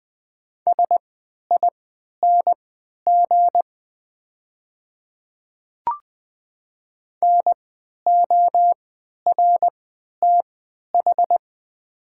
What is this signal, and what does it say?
Morse code, heard as a keyed tone, about 20 wpm, spaced as for 11 wpm, 700 Hz. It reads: SING NORTH
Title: 3x Character Spacing - Top 500 Words - 20wpm